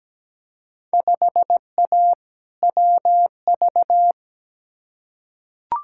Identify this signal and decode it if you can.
Morse code, heard as a keyed tone, about 17 wpm, 700 Hz. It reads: 5A WV
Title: Field Day Exchanges - 17wpm